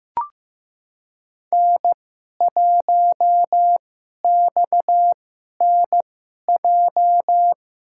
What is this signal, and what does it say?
Morse code, heard as a keyed tone, about 15 wpm, spaced as for 11 wpm, 700 Hz. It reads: N1XNJ
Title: USA Call Signs - 5 Characters - 2x Character Spacing - 15wpm